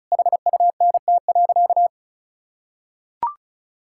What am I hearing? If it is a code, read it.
Morse code, heard as a keyed tone, about 36 wpm, 700 Hz. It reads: HUNT.
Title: Sentences from Top 1600 Words 35wpm